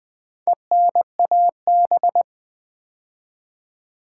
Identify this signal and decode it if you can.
Morse code, heard as a keyed tone, about 20 wpm, 700 Hz. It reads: ENAB
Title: Build A Word - 2 Letters At A Time - 6-Letter Words - 20wpm